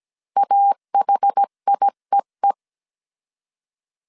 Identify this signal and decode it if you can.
Morse code, heard as a keyed tone, about 17 wpm, 800 Hz. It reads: AHIEE